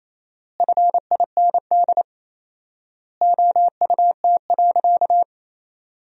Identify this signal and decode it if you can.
Morse code, heard as a keyed tone, about 28 wpm, 700 Hz. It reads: FIND OUT.
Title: 4x Word-Spacing - Sentences from Top 200 Words - 28wpm